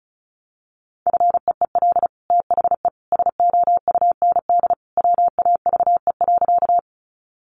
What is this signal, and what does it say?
Morse code, heard as a keyed tone, about 34 wpm, 700 Hz. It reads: FEEL THE SOUND WAVE.
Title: Sentences from Top 1400 Words 35wpm